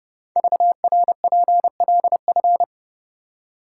Morse code, heard as a keyed tone, about 30 wpm, 700 Hz. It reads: VRPLF
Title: Code Groups - 5 Letters - 30wpm